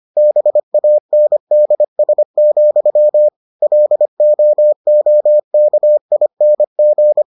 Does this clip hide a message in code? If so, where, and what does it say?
Yes: Morse code from 0.2 to 7.2 s, heard as a keyed tone, about 25 wpm, 600 Hz. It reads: BANDS, LOOKING